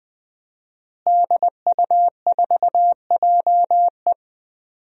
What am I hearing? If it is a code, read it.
Morse code, heard as a keyed tone, about 20 wpm, 700 Hz. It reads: DU4JE
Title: DXCC-100 Call Signs - 5 Characters - Repeated 3x - 20wpm